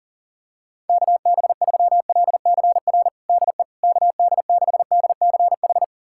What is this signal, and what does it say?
Morse code, heard as a keyed tone, about 40 wpm, 690 Hz. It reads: KB3LCR DE KD6DCH
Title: USA Callsign DE Callsign - 6 Characters - 40wpm